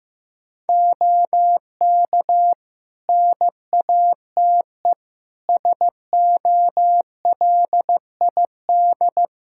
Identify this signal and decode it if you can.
Morse code, heard as a keyed tone, about 15 wpm, 700 Hz. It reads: OK NATE SOLID
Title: Practice QSO - Exchange 3 - More Info - 15wpm